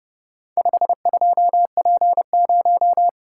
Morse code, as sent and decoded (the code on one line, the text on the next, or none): ..... ..--- .--. -----
52P0